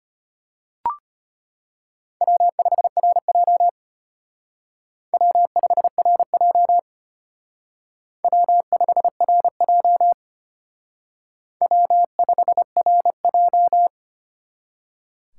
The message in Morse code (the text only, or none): W5RJ W5RJ W5RJ W5RJ